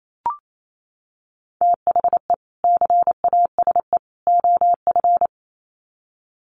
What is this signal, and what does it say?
Morse code, heard as a keyed tone, about 28 wpm, 700 Hz. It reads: THE CASE OF
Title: Sets of 3 Words 28wpm